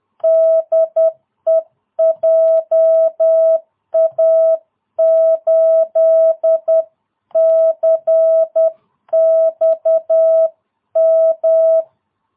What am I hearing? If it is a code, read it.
Morse code, heard as a keyed tone, about 10 wpm, 660 Hz. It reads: DE JA8CXM